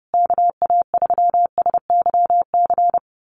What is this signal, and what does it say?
Morse code, heard as a keyed tone, about 30 wpm, 700 Hz. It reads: KA3SYC
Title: USA Call Signs Repeated 2x 30wpm